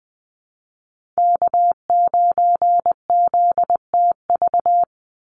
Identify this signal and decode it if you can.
Morse code, heard as a keyed tone, about 20 wpm, 700 Hz. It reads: K9ZTV
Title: Call Signs Repeated 2x - 20wpm